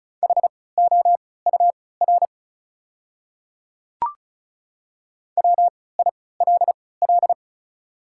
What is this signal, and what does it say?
Morse code, heard as a keyed tone, about 36 wpm, spaced as for 20 wpm, 690 Hz. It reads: HOUR WILL
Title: 3x Character Spacing - Top 500 Words - 35wpm